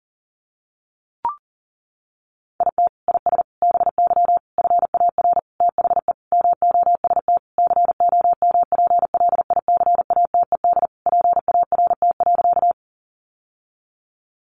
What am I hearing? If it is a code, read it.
Morse code, heard as a keyed tone, about 40 wpm, 700 Hz. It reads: IT IS BY FAR THE MOST COMPLICATED PART.
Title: Sentences from Top 1200 Words 40wpm